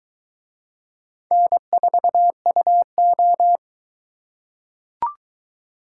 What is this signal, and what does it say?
Morse code, heard as a keyed tone, about 22 wpm, 700 Hz. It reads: N4UO